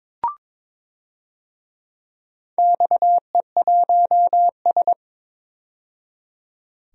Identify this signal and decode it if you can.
Morse code, heard as a keyed tone, about 22 wpm, 700 Hz. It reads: XE1S
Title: DXCC-100 Call Signs - 4 Characters - 22wpm